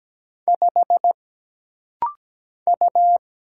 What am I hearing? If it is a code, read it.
Morse code, heard as a keyed tone, about 17 wpm, 700 Hz. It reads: 5 U
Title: Single Letter-Number - Rapid-Fire 17wpm